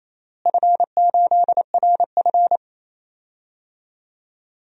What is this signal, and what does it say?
Morse code, heard as a keyed tone, about 27 wpm, 700 Hz. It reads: F8RF
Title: DXCC-100 Call Signs - 4 Characters - Repeated 2x - 28wpm